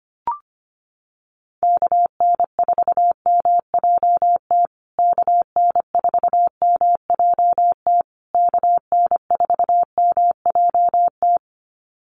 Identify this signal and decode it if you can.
Morse code, heard as a keyed tone, about 25 wpm, 700 Hz. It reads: KN4MJT KN4MJT KN4MJT
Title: USA Call Signs Repeated 3x 25wpm